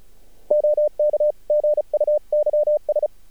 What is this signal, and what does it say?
Morse code, heard as a keyed tone, about 34 wpm, 600 Hz. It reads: OK GUYS
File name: archived / CRSnet